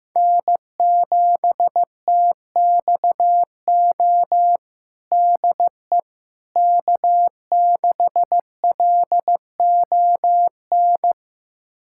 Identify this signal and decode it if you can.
Morse code, heard as a keyed tone, about 15 wpm, 700 Hz. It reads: N7TXO DE K6LON
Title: USA Callsign DE Callsign - 5 Characters - 15wpm